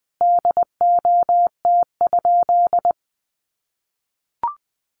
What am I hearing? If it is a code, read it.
Morse code, heard as a keyed tone, about 19 wpm, 700 Hz. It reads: DOT?
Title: Sentences from Top 1500 Words 20wpm